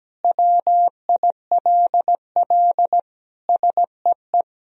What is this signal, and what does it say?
Morse code, heard as a keyed tone, about 17 wpm, 700 Hz. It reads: WILL SEE
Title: Sentences from Top 1000 Words - Review Part 4 - 17wpm